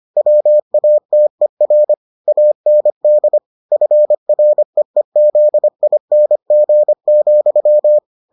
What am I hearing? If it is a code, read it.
Morse code, heard as a keyed tone, about 25 wpm, 600 Hz. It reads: WATER AND FREEZING,